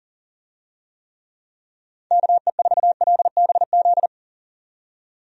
Morse code, heard as a keyed tone, about 40 wpm, 700 Hz. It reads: KE4LBZ
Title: USA Call Signs - 6 Characters - 40wpm